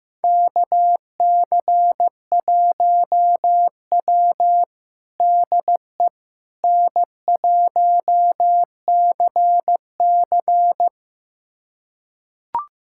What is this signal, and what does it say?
Morse code, heard as a keyed tone, about 15 wpm, 700 Hz. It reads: KC1W DE N1CC
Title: USA Callsign DE Callsign - 4 Characters - 15wpm